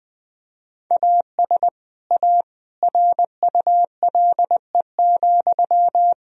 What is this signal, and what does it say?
Morse code, heard as a keyed tone, about 20 wpm, 700 Hz. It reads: AS A RULE,
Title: Sentences from Top 1000 Words - Review Part 2 - 20wpm